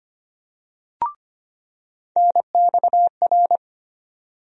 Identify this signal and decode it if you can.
Morse code, heard as a keyed tone, about 25 wpm, 700 Hz. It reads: NXR
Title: USA Call Sign Postfixes - 3 Letters 25wpm